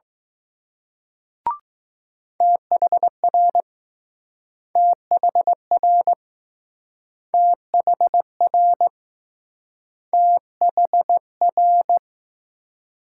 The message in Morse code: THR THR THR THR

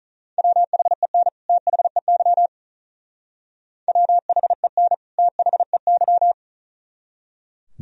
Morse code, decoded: WHEN THEY WHEN THEY

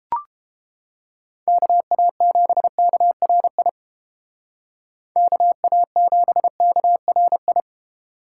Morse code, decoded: KA7KRI KA7KRI

